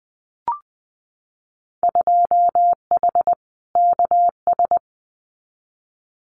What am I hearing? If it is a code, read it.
Morse code, heard as a keyed tone, about 20 wpm, 700 Hz. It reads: 2H KS
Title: Winter Field Day Exchanges Encore 20wpm